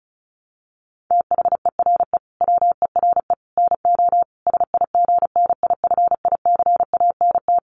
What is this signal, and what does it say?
Morse code, heard as a keyed tone, about 34 wpm, 700 Hz. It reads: THERE WERE NO SIGNIFICANT
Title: Sets of 5 Words Encore2 35wpm